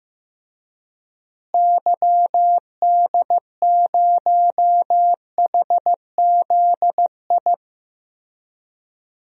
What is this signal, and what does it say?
Morse code, heard as a keyed tone, about 15 wpm, 700 Hz. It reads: YD0HZI
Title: DXCC-100 Call Signs - 6 Characters - Repeated 3x - 15wpm